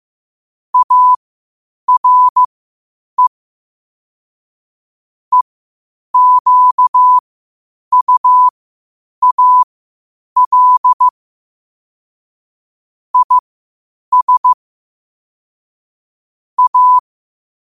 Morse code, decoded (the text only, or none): ARE EQUAL IS A